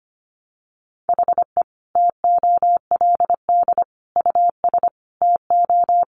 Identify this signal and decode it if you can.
Morse code, heard as a keyed tone, about 25 wpm, 700 Hz. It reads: HE TOLD US TO
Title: Sentences from Top 1000 Words 25wpm